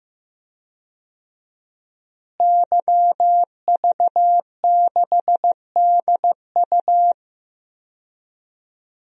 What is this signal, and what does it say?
Morse code, heard as a keyed tone, about 15 wpm, 700 Hz. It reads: YV6DU
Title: DXCC-100 Call Signs - 5 Characters - Repeated 3x - 15wpm